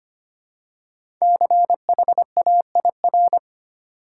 Morse code, heard as a keyed tone, about 30 wpm, 700 Hz. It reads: CHAIR
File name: Easy Words - Speed-Racing with Hints - 25wpm